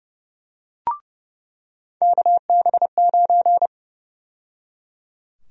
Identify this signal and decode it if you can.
Morse code, heard as a keyed tone, about 30 wpm, 700 Hz. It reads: KB9
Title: USA Call Sign Prefixes - 30wpm